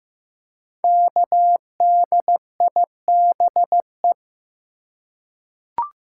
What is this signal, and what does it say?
Morse code, heard as a keyed tone, about 15 wpm, 700 Hz. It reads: KDIBE
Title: Code Groups - 5 Letters - 15wpm